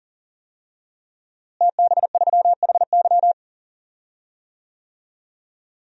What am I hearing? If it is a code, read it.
Morse code, heard as a keyed tone, about 40 wpm, 700 Hz. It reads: TB3HY
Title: DXCC-100 Call Signs - 5 Characters - Repeated 3x - 40wpm